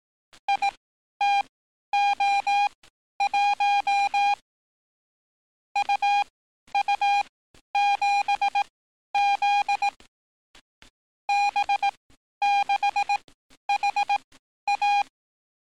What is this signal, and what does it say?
Morse code, heard as a keyed tone, about 18 wpm, spaced as for 11 wpm, 810 Hz. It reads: ITO1 UU7Z B6HA
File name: training / NAVPERS 11049to11060 / NAVPERS 11051 RA RB